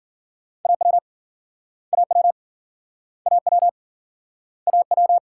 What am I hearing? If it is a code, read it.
Morse code, heard as a keyed tone, about 50 wpm, 700 Hz. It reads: AW AW AW AW